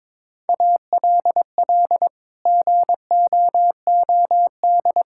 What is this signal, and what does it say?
Morse code, heard as a keyed tone, about 21 wpm, 700 Hz. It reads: ALL GOOD
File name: Sets of 9 Words - 22wpm